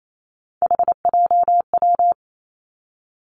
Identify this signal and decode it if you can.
Morse code, heard as a keyed tone, about 28 wpm, 700 Hz. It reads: HJW
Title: Three Challenging Letter-Number Combinations 28wpm